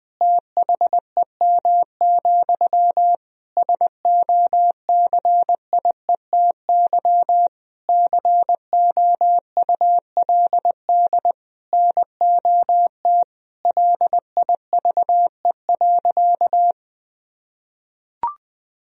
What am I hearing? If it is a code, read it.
Morse code, heard as a keyed tone, about 20 wpm, 700 Hz. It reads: THEM, SOCIETY COULD NOT LIVE.
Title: Sentences from Top 1000 Words - Review Part 4 - 20wpm